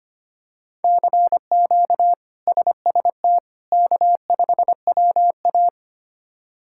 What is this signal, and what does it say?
Morse code, heard as a keyed tone, about 25 wpm, 700 Hz. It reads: CQ SST K5WA
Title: K1USN SST QSO Practice - 25wpm